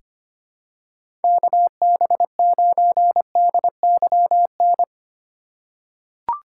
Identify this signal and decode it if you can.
Morse code, heard as a keyed tone, about 25 wpm, 700 Hz. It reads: KB9DYN
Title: USA Call Signs - 6 Characters - 2x Character Spacing - 25wpm